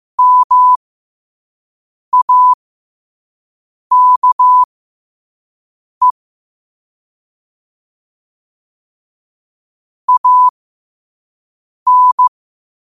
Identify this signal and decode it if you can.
Morse code, heard as a keyed tone, about 15 wpm, spaced as for 5 wpm, 1000 Hz. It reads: MAKE AN
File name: qotdfiles / 5wpm